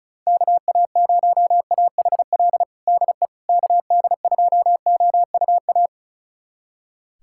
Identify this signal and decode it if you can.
Morse code, heard as a keyed tone, about 36 wpm, 700 Hz. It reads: KA0AHL DE KD2OUA
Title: USA Callsign DE Callsign 35wpm